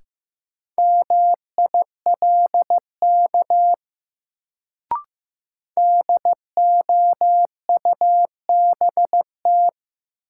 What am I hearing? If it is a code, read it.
Morse code, heard as a keyed tone, about 15 wpm, 700 Hz. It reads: MILK DOUBT
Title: Sentences from Top 1500 Words - New Word Drill - 15wpm